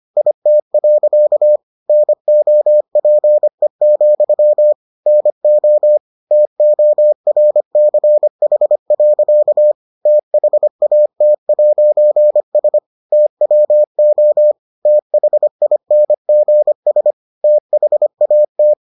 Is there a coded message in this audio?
Yes: Morse code from 0.2 to 18.7 s, heard as a keyed tone, about 25 wpm, 600 Hz. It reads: IT. NOPE, NO TORCH. THAT'S TWO THINGS THAT